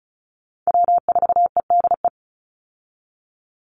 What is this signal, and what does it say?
Morse code, heard as a keyed tone, about 34 wpm, 700 Hz. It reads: W4EDE
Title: Call Signs - 35wpm